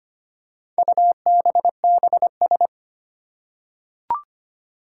Morse code, heard as a keyed tone, about 25 wpm, 700 Hz. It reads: UBBS
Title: Four Challenging Letter-Number Combinations 25wpm